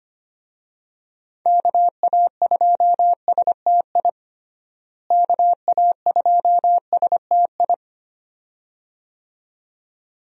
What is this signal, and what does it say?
Morse code, heard as a keyed tone, about 25 wpm, 700 Hz. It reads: KA2STI KA2STI
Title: USA Call Signs - 6 Characters - Repeated 2x - 25wpm